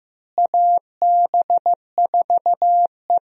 Morse code, heard as a keyed tone, about 15 wpm, 700 Hz. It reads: AB4E
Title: USA Call Signs - 4 Characters - Repeated 3x - 15wpm